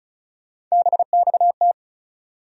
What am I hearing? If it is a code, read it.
Morse code, heard as a keyed tone, about 36 wpm, 700 Hz. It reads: BXT
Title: USA Call Sign Postfixes - 3 Letters 35wpm